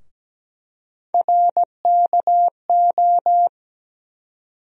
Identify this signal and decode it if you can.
Morse code, heard as a keyed tone, about 17 wpm, 700 Hz. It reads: RKO
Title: USA Call Sign Postfixes - 3 Letters 17wpm